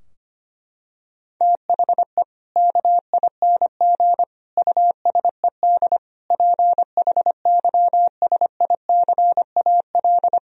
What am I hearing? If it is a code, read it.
Morse code, heard as a keyed tone, about 25 wpm, 700 Hz. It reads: THE KING USED PHYSICAL
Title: Sentences from Top 1000 Words - Review Part 4 - 25wpm